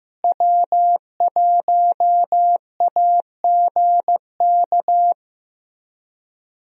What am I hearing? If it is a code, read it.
Morse code, heard as a keyed tone, about 16 wpm, 700 Hz. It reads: W1AGK
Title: USA Call Signs - 5 Characters - Repeated 2x - 15wpm